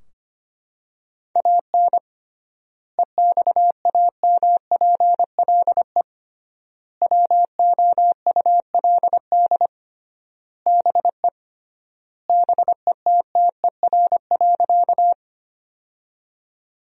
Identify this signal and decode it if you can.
Morse code, heard as a keyed tone, about 25 wpm, 700 Hz. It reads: AN EXAMPLE WOULD BE BETTER.
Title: 3x Word-Spacing - Sentences from Top 400 Words - 25wpm